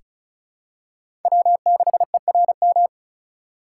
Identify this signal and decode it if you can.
Morse code, heard as a keyed tone, about 36 wpm, 700 Hz. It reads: W6ERM
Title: USA Call Signs - 5 Characters - 2x Character Spacing - 35wpm